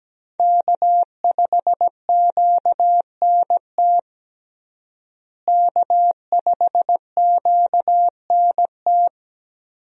Morse code, heard as a keyed tone, about 17 wpm, 700 Hz. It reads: K5QNT K5QNT
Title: USA Call Signs - 5 Characters - Repeated 2x - 17wpm